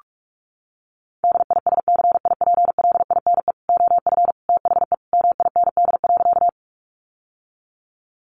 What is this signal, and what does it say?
Morse code, heard as a keyed tone, about 44 wpm, 700 Hz. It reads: DISCIPLINE OF THE MIND.
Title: Sentences from Top 900 Words 45wpm